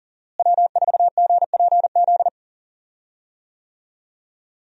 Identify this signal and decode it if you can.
Morse code, heard as a keyed tone, about 37 wpm, 700 Hz. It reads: W4GPZ
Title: USA Call Signs - 5 Characters - Repeated 3x - 40wpm